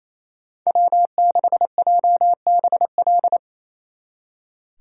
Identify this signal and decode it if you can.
Morse code, heard as a keyed tone, about 34 wpm, 700 Hz. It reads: W6JBL
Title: Speed Racing - USA 1x3 Call Signs 28wpm